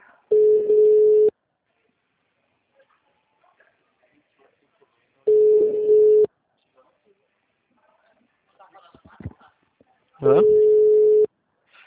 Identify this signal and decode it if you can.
Ringback tone (Europe, 425 Hz), with speech.
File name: out- -2006